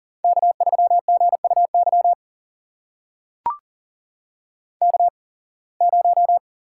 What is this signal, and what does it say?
Morse code, heard as a keyed tone, about 40 wpm, spaced as for 26 wpm, 700 Hz. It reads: K3GUY K0
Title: USA Call Signs - 5 Characters - 8x Character Spacing - 40wpm